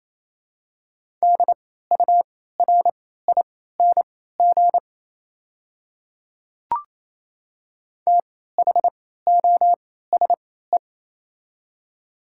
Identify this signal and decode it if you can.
Morse code, heard as a keyed tone, about 28 wpm, spaced as for 16 wpm, 700 Hz. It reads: DURING THOSE
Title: 3x Character Spacing - Top 500 Words - 28wpm